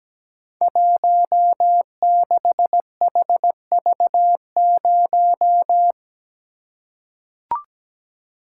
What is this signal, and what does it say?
Morse code, heard as a keyed tone, about 17 wpm, 700 Hz. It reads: 16HV0
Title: Code Groups - 5 Letter:Numbers - 17wpm